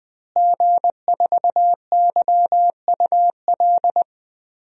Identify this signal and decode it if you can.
Morse code, heard as a keyed tone, about 20 wpm, 700 Hz. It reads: G4YUL
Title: DXCC-100 Call Signs - 5 Characters - Repeated 3x - 20wpm